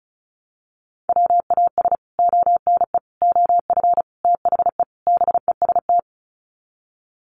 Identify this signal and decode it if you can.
Morse code, heard as a keyed tone, about 34 wpm, 700 Hz. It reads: WAS ONE OF THE BEST
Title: Sets of 5 Words Encore2 35wpm